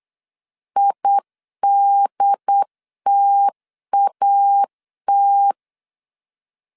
Morse code, heard as a keyed tone, about 8 wpm, 800 Hz. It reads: IDTAT